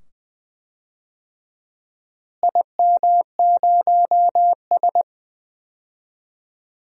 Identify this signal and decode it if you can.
Morse code, heard as a keyed tone, about 21 wpm, 700 Hz. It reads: IM0S
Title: DXCC-100 Call Signs - 4 Characters - Repeated 2x - 20wpm